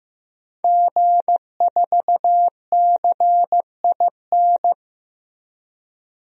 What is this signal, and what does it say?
Morse code, heard as a keyed tone, about 15 wpm, 700 Hz. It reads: G4CIN
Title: DXCC-100 Call Signs - 5 Characters - Repeated 2x - 15wpm